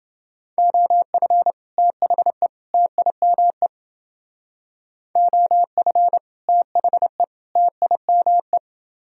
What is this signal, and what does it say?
Morse code, heard as a keyed tone, about 29 wpm, 700 Hz. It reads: OF THE TIME OF THE TIME